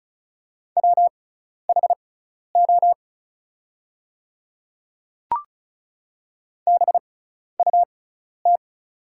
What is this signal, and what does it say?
Morse code, heard as a keyed tone, about 36 wpm, spaced as for 12 wpm, 700 Hz. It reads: WHO BUT
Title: 6x Character Spacing - Top 100 Words - 35wpm